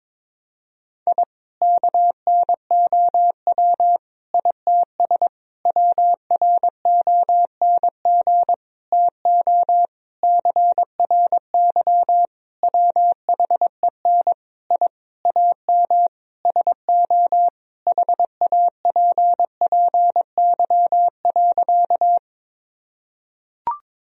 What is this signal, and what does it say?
Morse code, heard as a keyed tone, about 21 wpm, 700 Hz. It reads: I KNOW ITS WRONG TO CRY WHEN I AM SO HAPPY.
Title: Sentences from Top 1000 Words - Review Part 2 - 22wpm